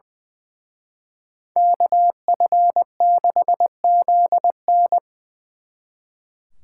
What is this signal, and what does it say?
Morse code, heard as a keyed tone, about 24 wpm, 700 Hz. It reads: KF6ZN
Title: Speed Racing - USA Call Signs 20wpm